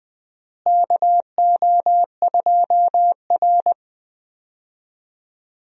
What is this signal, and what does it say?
Morse code, heard as a keyed tone, about 20 wpm, 700 Hz. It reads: KO2R
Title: USA Call Signs - 4 Characters - Repeated 3x - 20wpm